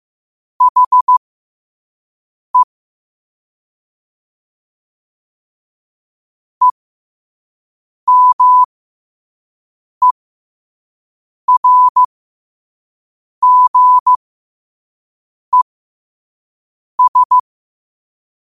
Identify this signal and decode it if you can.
Morse code, heard as a keyed tone, about 15 wpm, spaced as for 5 wpm, 1000 Hz. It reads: HE EMERGES
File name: qotdfiles / 5wpm